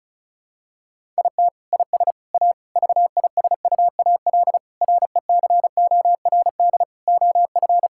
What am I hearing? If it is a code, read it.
Morse code, heard as a keyed tone, about 36 wpm, 700 Hz. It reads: IT IS A VISUAL RECORD OF
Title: Sentences from Top 1000 Words - Review Part 6 - 35wpm